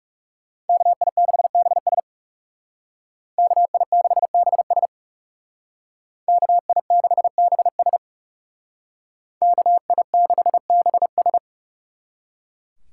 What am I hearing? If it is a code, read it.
Morse code, heard as a keyed tone, about 37 wpm, 700 Hz. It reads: KI6BS KI6BS KI6BS KI6BS
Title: Speed Racing - USA 2x2 Call Signs 30wpm